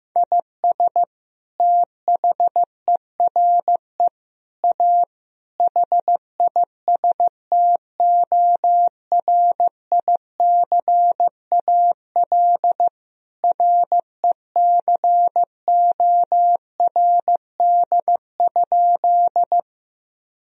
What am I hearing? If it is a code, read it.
Morse code, heard as a keyed tone, about 15 wpm, 700 Hz. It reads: IS THERE A HISTORICAL RECORD?
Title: Sentences from Top 1000 Words - Review Part 4 - 15wpm